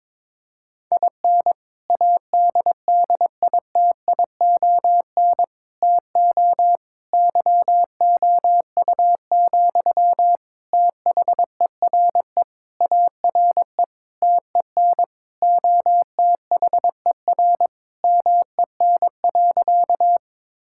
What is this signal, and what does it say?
Morse code, heard as a keyed tone, about 22 wpm, 700 Hz. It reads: IN ADDITION TO YOU, THERE ARE TEN OTHER MEN.
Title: Sentences from Top 1000 Words - Review Part 6 - 22wpm